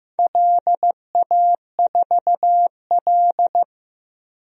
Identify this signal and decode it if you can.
Morse code, heard as a keyed tone, about 15 wpm, 700 Hz. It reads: LA4L
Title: DXCC-100 Call Signs - 4 Characters - Repeated 3x - 15wpm